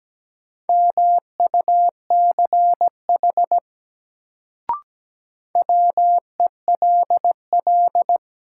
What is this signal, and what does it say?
Morse code, heard as a keyed tone, about 17 wpm, 700 Hz. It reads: MUCH WELL
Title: Sentences from Top 200 Words - New Word Drill - 17wpm